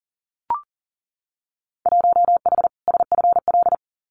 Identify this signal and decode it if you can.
Morse code, heard as a keyed tone, about 42 wpm, 700 Hz. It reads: 1H SFL
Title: Winter Field Day Exchanges 40wpm